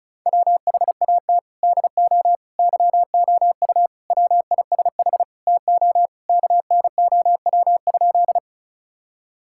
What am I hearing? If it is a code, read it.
Morse code, heard as a keyed tone, about 34 wpm, 700 Hz. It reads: WHAT DO YOU WISH TO KNOW?
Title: Sentences from Top 1600 Words 35wpm